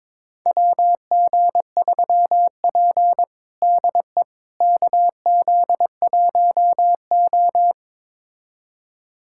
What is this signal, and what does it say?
Morse code, heard as a keyed tone, about 22 wpm, 700 Hz. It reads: WG3P DE KZ1O
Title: USA Callsign DE Callsign - 4 Characters - 22wpm